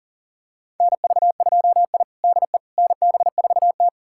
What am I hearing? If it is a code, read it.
Morse code, heard as a keyed tone, about 40 wpm, 700 Hz. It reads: NV2I DE NB4T
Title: USA Callsign DE Callsign - 4 Characters - 40wpm